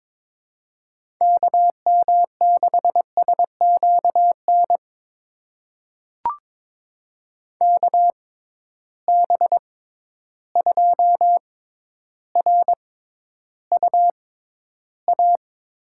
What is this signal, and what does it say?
Morse code, heard as a keyed tone, about 22 wpm, spaced as for 11 wpm, 700 Hz. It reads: KM6SQN KB2RUA
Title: USA Call Signs - 6 Characters - 6x Character Spacing - 22wpm